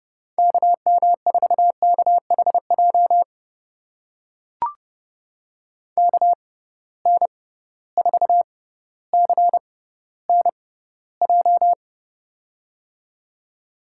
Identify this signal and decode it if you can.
Morse code, heard as a keyed tone, about 30 wpm, spaced as for 15 wpm, 700 Hz. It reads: KM4KHJ KN4CNJ
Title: USA Call Signs - 6 Characters - 6x Character Spacing - 30wpm